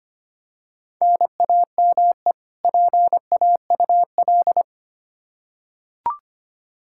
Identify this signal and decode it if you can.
Morse code, heard as a keyed tone, about 25 wpm, 700 Hz. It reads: NAME PAUL
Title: Practice QSO - Exchange 1 - Beginning - 25wpm